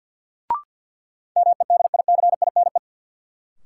Morse code, heard as a keyed tone, about 53 wpm, 700 Hz. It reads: MEDICINE
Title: Sentences from Top 900 Words - New Word Drill - 50wpm